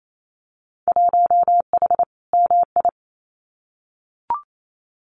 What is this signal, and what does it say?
Morse code, heard as a keyed tone, about 28 wpm, 700 Hz. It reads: 1H MI
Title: Winter Field Day Exchanges 28wpm